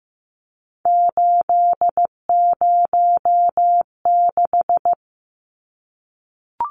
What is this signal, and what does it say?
Morse code, heard as a keyed tone, about 15 wpm, 700 Hz. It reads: 806
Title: Numbers - 3-Digit 15wpm